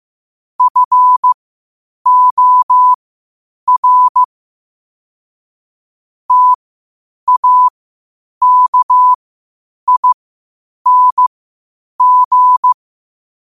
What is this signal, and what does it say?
Morse code, heard as a keyed tone, about 15 wpm, spaced as for 8 wpm, 1000 Hz. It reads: FOR TAKING
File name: qotdfiles / 8wpm